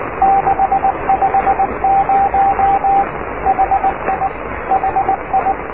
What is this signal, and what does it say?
Morse code, heard as a keyed tone, about 19 wpm, 750 Hz, with radio noise. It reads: 650 HI HI
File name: oscar 7 de k5dze section 2